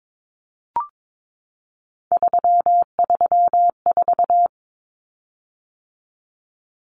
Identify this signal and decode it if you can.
Morse code, heard as a keyed tone, about 22 wpm, 700 Hz. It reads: 334
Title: Common Cut Numbers - 3-Digit 22wpm